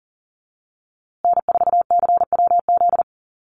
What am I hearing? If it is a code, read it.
Morse code, heard as a keyed tone, about 40 wpm, 700 Hz. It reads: N4CWZ
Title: USA Call Signs Repeated 3x 40wpm